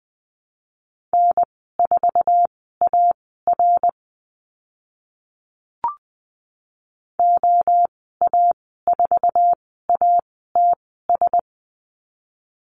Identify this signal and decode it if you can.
Morse code, heard as a keyed tone, about 20 wpm, spaced as for 14 wpm, 700 Hz. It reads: N4AR OA4ATS
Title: 062 Lesson 11g - Letter h - Call Signs 20wpm